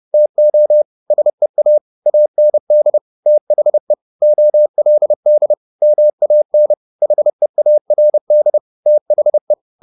Morse code, heard as a keyed tone, about 30 wpm, 600 Hz. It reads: TO SEA AND THE OLD MAN HEARD THE
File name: the old man and the sea / the old man and the sea 1 30wpm